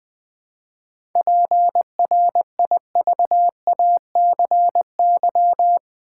Morse code, heard as a keyed tone, about 20 wpm, 700 Hz. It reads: PRIVACY
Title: 7 Letter Words - First 3 Letters Given - 20wpm